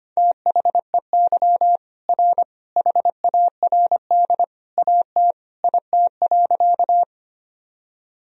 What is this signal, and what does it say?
Morse code, heard as a keyed tone, about 25 wpm, 700 Hz. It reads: THEY R HARD AT IT.
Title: Sentences from Top 300 Words and Abbreviations 25wpm